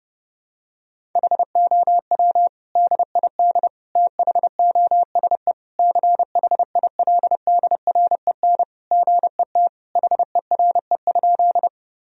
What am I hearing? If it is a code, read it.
Morse code, heard as a keyed tone, about 32 wpm, 700 Hz. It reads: HOW DID THOSE CHILDREN GET HERE?
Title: Sentences from Top 1000 Words - Review Part 1 - 30wpm